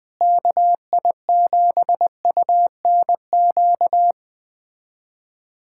Morse code, heard as a keyed tone, about 21 wpm, 700 Hz. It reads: KI7UNQ
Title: USA Callsign DE Callsign - 6 Characters - 20wpm